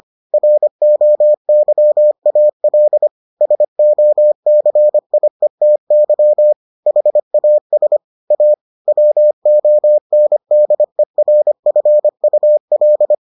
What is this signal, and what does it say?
Morse code, heard as a keyed tone, about 25 wpm, 600 Hz. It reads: ROYAL SOCIETY HAS A WONDERFUL